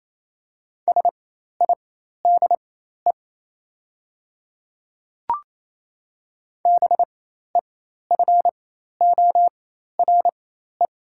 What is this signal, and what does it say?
Morse code, heard as a keyed tone, about 28 wpm, spaced as for 13 wpm, 700 Hz. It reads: SIDE BEFORE
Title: 4x Character Spacing - Top 300 Words - 28wpm